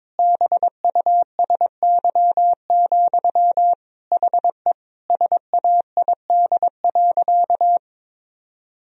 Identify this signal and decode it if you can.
Morse code, heard as a keyed tone, about 22 wpm, 700 Hz. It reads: BUSY, HE SAID.
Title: Sentences from Top 1000 Words - Review Part 3 - 22wpm